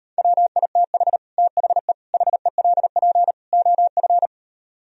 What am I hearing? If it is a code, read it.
Morse code, heard as a keyed tone, about 38 wpm, 700 Hz. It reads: WITH THE HELP OF